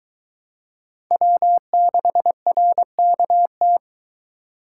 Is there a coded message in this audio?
Yes: Morse code from 1.1 to 3.8 s, heard as a keyed tone, about 22 wpm, 700 Hz. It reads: W6RKT